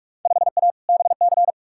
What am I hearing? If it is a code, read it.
Morse code, heard as a keyed tone, about 47 wpm, 700 Hz. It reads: 5A BC
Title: Field Day Exchanges - 45wpm